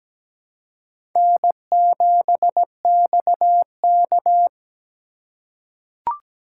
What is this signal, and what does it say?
Morse code, heard as a keyed tone, about 17 wpm, 700 Hz. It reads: N7XK
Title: USA Call Signs - 2x Character Spacing - 17wpm